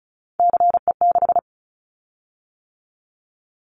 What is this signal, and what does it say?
Morse code, heard as a keyed tone, about 34 wpm, 700 Hz. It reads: CE6
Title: Callsign Prefixes for Top 100 DXCC - 35wpm